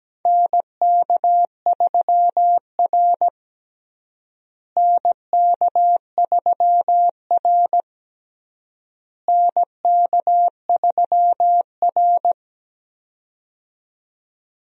Morse code, heard as a keyed tone, about 17 wpm, 700 Hz. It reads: NK3R NK3R NK3R
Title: USA Call Signs - 4 Characters - Repeated 3x - 17wpm